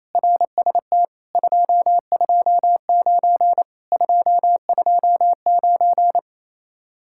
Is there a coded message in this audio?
Yes: Morse code from 0.2 to 6.2 s, heard as a keyed tone, about 37 wpm, 700 Hz. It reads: RST 229 229